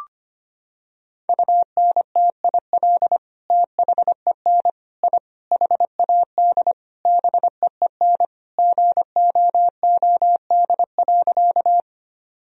Morse code, heard as a keyed tone, about 25 wpm, 700 Hz. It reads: UNTIL THEN I HAD BEEN GOOD.
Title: Sentences from Top 1000 Words - Review Part 1 - 25wpm